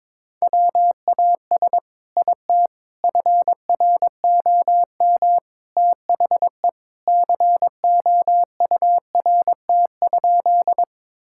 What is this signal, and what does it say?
Morse code, heard as a keyed tone, about 23 wpm, 700 Hz. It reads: WAS IT FROM THE COURT?
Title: Sentences from Top 1600 Words 22wpm